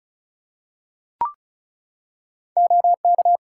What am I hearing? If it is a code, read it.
Morse code, heard as a keyed tone, about 36 wpm, 700 Hz. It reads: OK
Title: Practice QSO - Exchange 3 - More Info - 35wpm